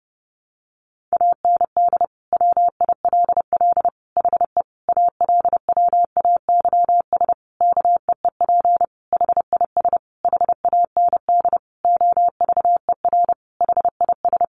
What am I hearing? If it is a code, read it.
Morse code, heard as a keyed tone, about 30 wpm, 700 Hz. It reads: AND WILL HE ALWAYS KEEP HIS HAND OVER HIS
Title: Sentences from Top 1200 Words 30wpm